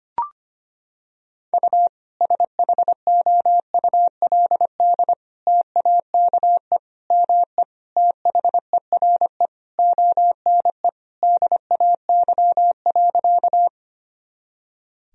Morse code, heard as a keyed tone, about 25 wpm, 700 Hz. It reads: U SHOULD TAKE ME THERE ONE DAY.
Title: Sentences from Top 300 Words and Abbreviations 25wpm